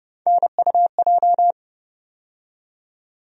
Morse code, heard as a keyed tone, about 30 wpm, 700 Hz. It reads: NUJ
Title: USA Call Sign Postfixes - 3 Letters 30wpm